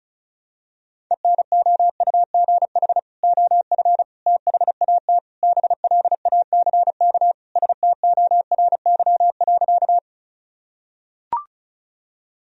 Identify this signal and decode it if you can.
Morse code, heard as a keyed tone, about 36 wpm, 700 Hz. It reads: ENOUGH OF THAT BLACK STORY.
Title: Sentences from Top 1000 Words - Review Part 2 - 35wpm